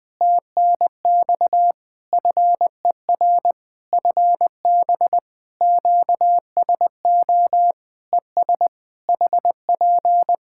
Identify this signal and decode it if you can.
Morse code, heard as a keyed tone, about 20 wpm, 700 Hz. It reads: TNX FER FB QSO ES HP
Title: Practice QSO - Exchange 4 - Ending - 20wpm